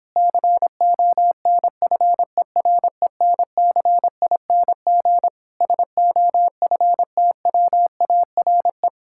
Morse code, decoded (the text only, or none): CONFERENCING SOFTWARE